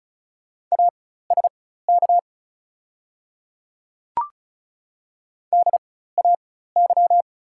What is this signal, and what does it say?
Morse code, heard as a keyed tone, about 36 wpm, spaced as for 16 wpm, 700 Hz. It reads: ASK DAY
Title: 4x Character Spacing - Top 200 Words - 35wpm